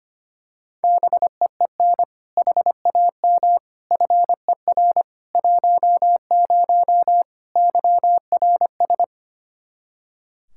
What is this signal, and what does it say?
Morse code, heard as a keyed tone, about 25 wpm, 700 Hz. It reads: BEEN HAM FER 10 YRS
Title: Practice QSO - Exchange 3 - More Info - 25wpm